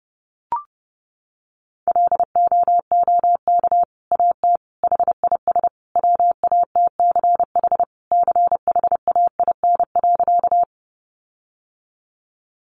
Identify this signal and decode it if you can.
Morse code, heard as a keyed tone, about 30 wpm, 700 Hz. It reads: LOOK AT HIS WATCH CHAIN.
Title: Sentences from Top 1500 Words 30wpm